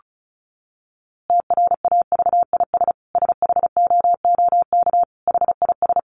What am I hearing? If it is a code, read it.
Morse code, heard as a keyed tone, about 34 wpm, 700 Hz. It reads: TRAVIS SHOOK HIS
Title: A Sound Of Thunder 35wpm